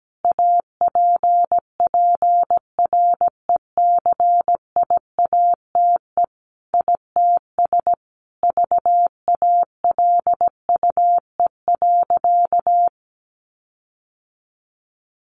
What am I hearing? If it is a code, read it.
Morse code, heard as a keyed tone, about 17 wpm, 700 Hz. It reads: APPRECIATE ITS VALUE.
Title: Sentences from Top 1300 Words 17wpm